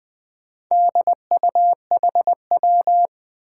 Morse code, heard as a keyed tone, about 20 wpm, 700 Hz. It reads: DUHW
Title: Four Challenging Letter-Number Combinations 20wpm